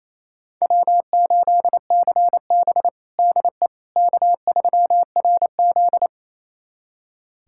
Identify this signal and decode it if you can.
Morse code, heard as a keyed tone, about 28 wpm, 700 Hz. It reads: W8CB DE K3RZ
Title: USA Callsign DE Callsign - 4 Characters - 28wpm